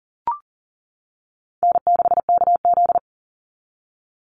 Morse code, heard as a keyed tone, about 40 wpm, 700 Hz. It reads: N6KZ
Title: NA SOTA Chaser Callsigns 40wpm